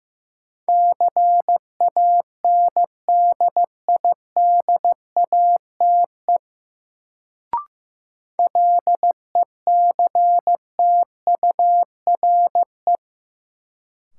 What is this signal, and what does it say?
Morse code, heard as a keyed tone, about 15 wpm, 700 Hz. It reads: CANDIDATE LECTURE
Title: Sentences from Top 1600 Words - New Word Drill Alt - 15wpm